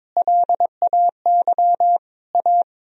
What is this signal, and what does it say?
Morse code, heard as a keyed tone, about 22 wpm, 700 Hz. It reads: LAY A
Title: Sentences from Top 1000 Words - Review Part 3 - 22wpm